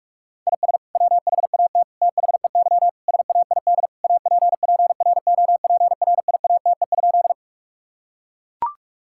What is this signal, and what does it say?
Morse code, heard as a keyed tone, about 42 wpm, 700 Hz. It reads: IS WHAT THEY SAID APPROPRIATE?
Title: Sentences from Top 1000 Words - Review Part 5 - 45wpm